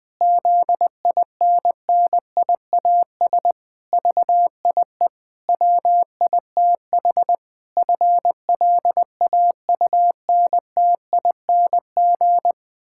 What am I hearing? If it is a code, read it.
Morse code, heard as a keyed tone, about 20 wpm, 700 Hz. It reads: ZINNIAS VIE WITH FLAUNTING
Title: Pangram Challenge v2 - 20wpm